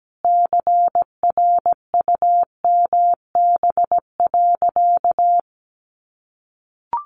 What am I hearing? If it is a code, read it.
Morse code, heard as a keyed tone, about 17 wpm, 700 Hz. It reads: CRUMB.
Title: Easy Phrases 17wpm